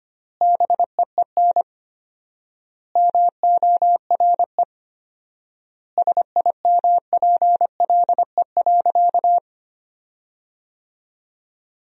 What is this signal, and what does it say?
Morse code, heard as a keyed tone, about 25 wpm, 700 Hz. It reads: BEEN MORE SIMPLE.
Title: 4x Word-Spacing - Sentences from Top 400 Words - 25wpm